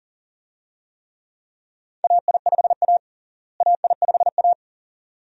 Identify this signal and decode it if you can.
Morse code, heard as a keyed tone, about 40 wpm, 700 Hz. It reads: AI5A AI5A
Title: USA Call Signs - 4 Characters - Repeated 2x - 40wpm